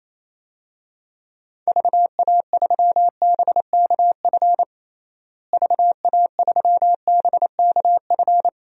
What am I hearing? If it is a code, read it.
Morse code, heard as a keyed tone, about 28 wpm, 700 Hz. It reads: VA3BKF VA3BKF
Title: DXCC-100 Call Signs - 6 Characters - Repeated 3x - 28wpm